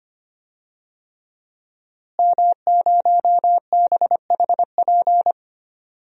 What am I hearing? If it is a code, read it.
Morse code, heard as a keyed tone, about 25 wpm, 700 Hz. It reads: M0BHP
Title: DXCC-100 Call Signs - 5 Characters - Repeated 3x - 25wpm